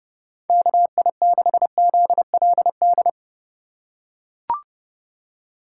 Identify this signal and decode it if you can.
Morse code, heard as a keyed tone, about 29 wpm, 700 Hz. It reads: KI6ZLD
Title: USA Call Signs - 3x Character Spacing - 30wpm